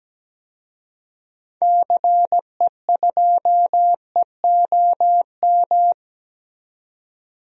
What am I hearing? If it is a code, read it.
Morse code, heard as a keyed tone, about 17 wpm, 700 Hz. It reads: CE2EOM
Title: DXCC-100 Call Signs - 6 Characters - 17wpm